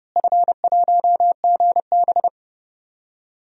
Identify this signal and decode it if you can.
Morse code, heard as a keyed tone, about 30 wpm, 700 Hz. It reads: F1GB
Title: DXCC-100 Call Signs - 4 Characters - Repeated 2x - 30wpm